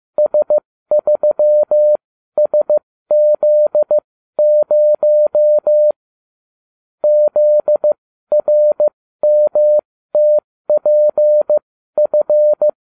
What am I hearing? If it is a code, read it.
Morse code, heard as a keyed tone, about 15 wpm, spaced as for 13 wpm, 600 Hz. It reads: S3SZ0 ZRMTPF